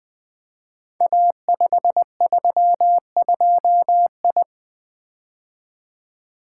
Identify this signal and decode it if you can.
Morse code, heard as a keyed tone, about 20 wpm, 700 Hz. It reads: A532I
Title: Code Groups - 5 Letter:Numbers - 20wpm